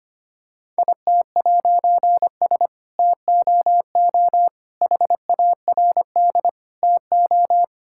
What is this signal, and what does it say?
Morse code, heard as a keyed tone, about 25 wpm, 700 Hz. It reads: IT'S TOO HARD TO
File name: practice / 25wpm / All-Age Appropriate Jokes - 25wpm